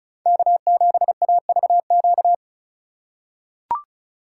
Morse code, heard as a keyed tone, about 36 wpm, 700 Hz. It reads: K7AVQ
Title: USA Call Signs - 5 Characters - 6x Character Spacing - 35wpm